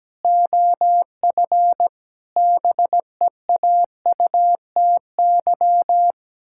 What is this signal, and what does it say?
Morse code, heard as a keyed tone, about 17 wpm, 700 Hz. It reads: OF BEAUTY
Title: Pangram Challenge v2 - 17wpm